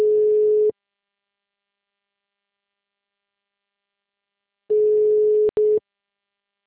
Ringback tone (Europe, 425 Hz).
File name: out- -2028